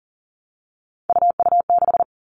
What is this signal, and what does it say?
Morse code, heard as a keyed tone, about 40 wpm, 700 Hz. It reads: UU6
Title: All Cut Numbers - 3-Digit 40wpm